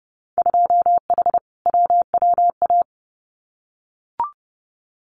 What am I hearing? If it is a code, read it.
Morse code, heard as a keyed tone, about 30 wpm, 700 Hz. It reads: 2H WWA
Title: Winter Field Day Exchanges 30wpm